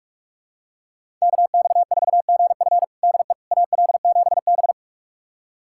Morse code, heard as a keyed tone, about 46 wpm, 700 Hz. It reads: KX4GF DE AL7B
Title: USA Callsign DE Callsign 45wpm